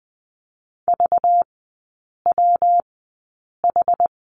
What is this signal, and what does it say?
Morse code, heard as a keyed tone, about 20 wpm, spaced as for 8 wpm, 700 Hz. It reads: VWH
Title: 191 Lesson 31e - Letter v - Three Characters 20wpm